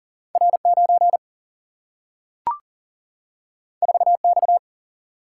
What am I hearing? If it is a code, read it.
Morse code, heard as a keyed tone, about 39 wpm, 700 Hz. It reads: R9 4X
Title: Single-Letter Single-Number 40wpm